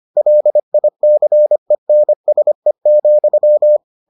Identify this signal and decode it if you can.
Morse code, heard as a keyed tone, about 25 wpm, 600 Hz. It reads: LICENSE,